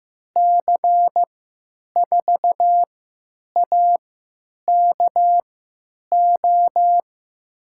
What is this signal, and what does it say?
Morse code, heard as a keyed tone, about 15 wpm, spaced as for 9 wpm, 700 Hz. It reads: C4AKO
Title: USA Call Signs - 6 Characters - 3x Character Spacing - 15wpm